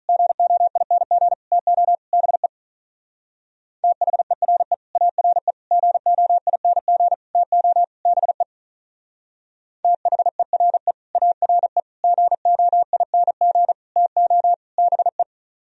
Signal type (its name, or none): Morse code